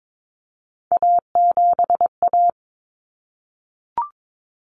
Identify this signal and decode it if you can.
Morse code, heard as a keyed tone, about 22 wpm, 700 Hz. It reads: A7A
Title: Common Cut Numbers - 3-Digit 22wpm